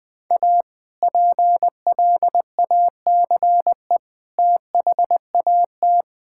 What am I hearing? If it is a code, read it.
Morse code, heard as a keyed tone, about 20 wpm, 700 Hz. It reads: A PLACE THAT
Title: Sentences with 2-Word Phrases - 20wpm